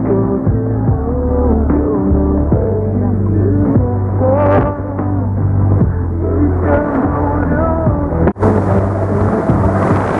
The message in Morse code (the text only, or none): none